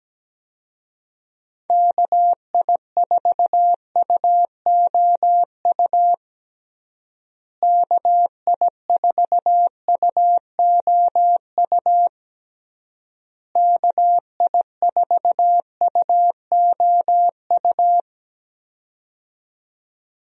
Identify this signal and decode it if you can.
Morse code, heard as a keyed tone, about 17 wpm, 700 Hz. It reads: KI4UOU KI4UOU KI4UOU
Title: USA Call Signs - 6 Characters - Repeated 3x - 17wpm